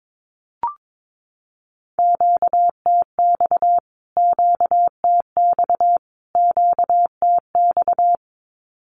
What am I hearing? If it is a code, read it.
Morse code, heard as a keyed tone, about 22 wpm, 700 Hz. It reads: QTX QTX QTX
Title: Q-Codes Repeated 3x 22wpm